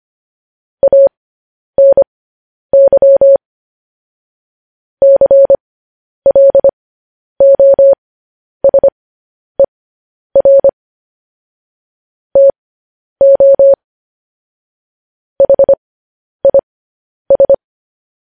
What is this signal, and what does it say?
Morse code, heard as a keyed tone, about 25 wpm, spaced as for 10 wpm, 550 Hz. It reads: ANY CLOSER TO HIS
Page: news headlines in morse code at 10 wpm